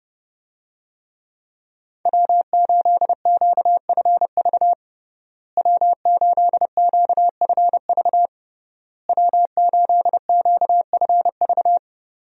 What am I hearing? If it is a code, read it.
Morse code, heard as a keyed tone, about 30 wpm, 700 Hz. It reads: W8QFV W8QFV W8QFV
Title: USA Call Signs - 5 Characters - Repeated 3x - 30wpm